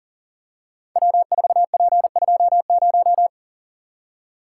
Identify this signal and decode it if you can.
Morse code, heard as a keyed tone, about 42 wpm, 700 Hz. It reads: W4P20
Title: Code Groups - 5 Letter:Numbers - 40wpm